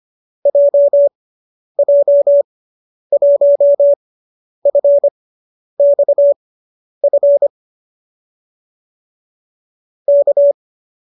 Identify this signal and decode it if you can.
Morse code, heard as a keyed tone, about 25 wpm, spaced as for 10 wpm, 570 Hz. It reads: JJ1FXF K
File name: qso005 10